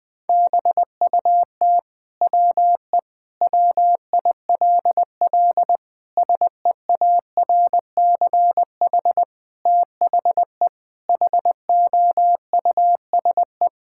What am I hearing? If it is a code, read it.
Morse code, heard as a keyed tone, about 20 wpm, 700 Hz. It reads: BUT WE WILL SEARCH THE HOUSE
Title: Sentences from Top 1000 Words - Review Part 5 - 20wpm